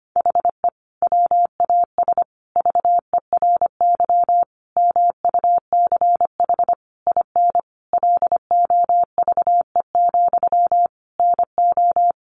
Morse code, heard as a keyed tone, about 25 wpm, 700 Hz. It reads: HE WAS VERY MUCH IN LOVE, NO
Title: Sentences from Top 1500 Words 25wpm